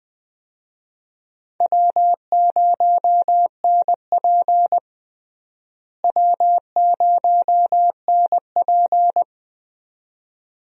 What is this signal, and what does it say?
Morse code, heard as a keyed tone, about 21 wpm, 700 Hz. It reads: W0NP W0NP
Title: USA Call Signs - 4 Characters - Repeated 2x - 20wpm